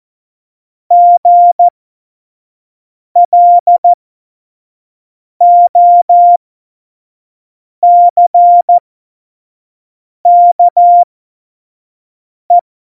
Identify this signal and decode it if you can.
Morse code, heard as a keyed tone, about 14 wpm, spaced as for 5 wpm, 700 Hz. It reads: GLOCKE
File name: Morsekurs / Witze / Witz 09. Oktober 2018, 25 F70